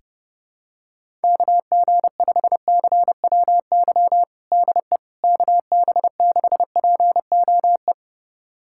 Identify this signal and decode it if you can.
Morse code, heard as a keyed tone, about 31 wpm, 700 Hz. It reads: KG5CWY DE KB6POE
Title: USA Callsign DE Callsign - 6 Characters - 30wpm